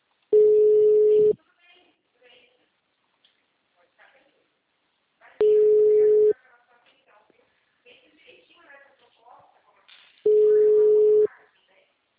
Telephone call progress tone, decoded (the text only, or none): ringback tone